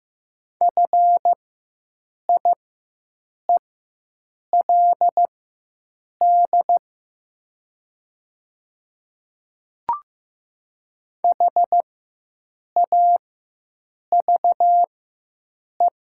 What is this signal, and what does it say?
Morse code, heard as a keyed tone, about 15 wpm, spaced as for 7 wpm, 700 Hz. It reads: FIELD HAVE
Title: 4x Character Spacing - Top 500 Words - 15wpm